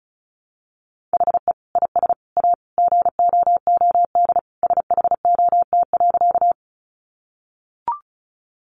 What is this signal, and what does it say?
Morse code, heard as a keyed tone, about 34 wpm, 700 Hz. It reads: HE IS A GOOD SHOT.
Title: Sentences from Top 1100 Words 35wpm